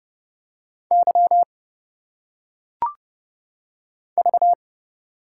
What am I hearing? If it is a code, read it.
Morse code, heard as a keyed tone, about 30 wpm, 700 Hz. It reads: Y V
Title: Single Letters - Phonetic Pronunciation 30wpm